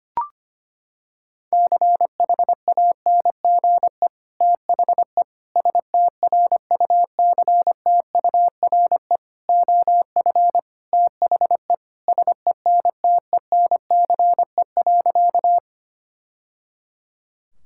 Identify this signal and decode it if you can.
Morse code, heard as a keyed tone, about 25 wpm, 700 Hz. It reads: CHANGE THE STRUCTURE OF THE SENTENCE.
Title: Sentences from Top 1000 Words - Review Part 6 - 25wpm